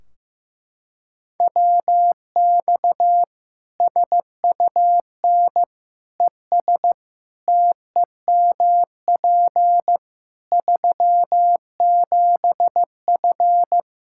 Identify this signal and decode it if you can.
Morse code, heard as a keyed tone, about 15 wpm, 700 Hz. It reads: WX SUN ES TEMP 37F
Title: K1USN SST QSO Practice 15wpm